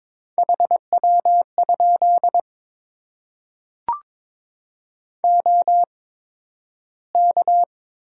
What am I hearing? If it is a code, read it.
Morse code, heard as a keyed tone, about 22 wpm, spaced as for 12 wpm, 700 Hz. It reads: HW? OK
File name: Top 90 QSO Elements - 8x Character Spacing - 22wpm